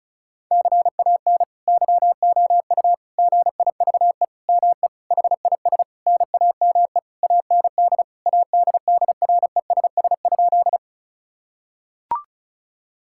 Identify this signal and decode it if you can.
Morse code, heard as a keyed tone, about 36 wpm, 700 Hz. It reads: CAN YOU GIVE ME HIS NAME AND ADDRESS?
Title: Sentences from Top 1000 Words - Review Part 6 - 35wpm